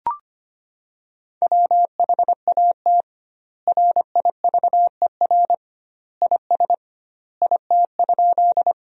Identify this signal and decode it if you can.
Morse code, heard as a keyed tone, about 25 wpm, 700 Hz. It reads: WHAT RIVER IS IT?
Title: 2x Word-Spacing - Sentences from Top 300 Words - 25wpm